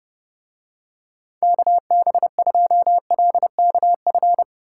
Morse code, heard as a keyed tone, about 32 wpm, 700 Hz. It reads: KB2LKF